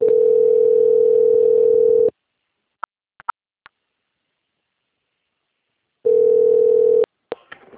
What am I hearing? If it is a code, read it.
Ringback tone (North America), 440 + 480 Hz.
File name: out- - - - - .15057